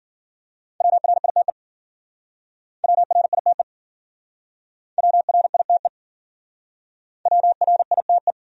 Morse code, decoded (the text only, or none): WRITE WRITE WRITE WRITE